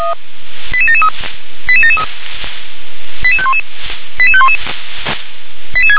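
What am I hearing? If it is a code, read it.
Five-tone selective call (ZVEI), heard as quick runs of five tones, tones of about 70 ms. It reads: 88773 88773 88511 88511
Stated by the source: ZVEI five-tone selective call 88743 received by SDR.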